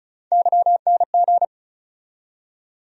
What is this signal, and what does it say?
Morse code, heard as a keyed tone, about 34 wpm, 700 Hz. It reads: YNG
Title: USA Call Sign Postfixes - 3 Letters 35wpm